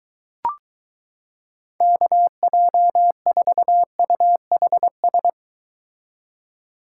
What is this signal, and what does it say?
Morse code, heard as a keyed tone, about 23 wpm, 700 Hz. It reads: KJ4UHS